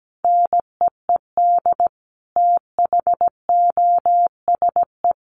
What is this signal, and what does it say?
Morse code, heard as a keyed tone, about 17 wpm, 700 Hz. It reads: NEED THOSE
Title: A Sound Of Thunder 17wpm